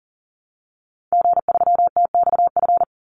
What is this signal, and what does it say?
Morse code, heard as a keyed tone, about 40 wpm, 700 Hz. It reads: G3TXF
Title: Call Signs Repeated 3x - 40wpm